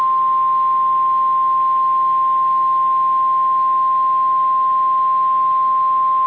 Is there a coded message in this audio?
Yes: Emergency Alert System (SAME) tones (the 1050 Hz attention tone) from 0.4 to 5.2 s.